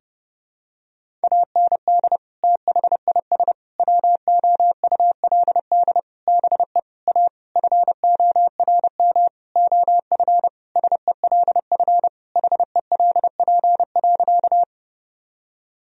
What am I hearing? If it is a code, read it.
Morse code, heard as a keyed tone, about 30 wpm, 700 Hz. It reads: AND THIS WOULD BE A FORM OF SELF HELP.
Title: Sentences from Top 1000 Words - Review Part 1 - 30wpm